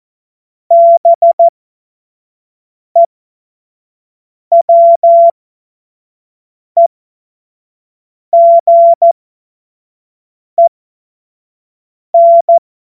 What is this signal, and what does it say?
Morse code, heard as a keyed tone, about 14 wpm, spaced as for 5 wpm, 680 Hz. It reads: BEWEGEN